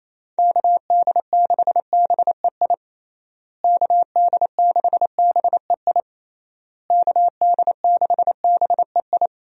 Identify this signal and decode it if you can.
Morse code, heard as a keyed tone, about 28 wpm, 700 Hz. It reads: KD6BEI KD6BEI KD6BEI
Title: USA Call Signs - 6 Characters - Repeated 3x - 28wpm